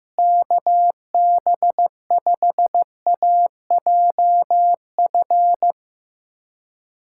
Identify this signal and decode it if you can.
Morse code, heard as a keyed tone, about 15 wpm, 700 Hz. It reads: KB5AJF
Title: USA Callsign DE Callsign 15wpm